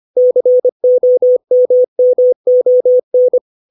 Morse code, heard as a keyed tone, about 25 wpm, 500 Hz. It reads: COMMON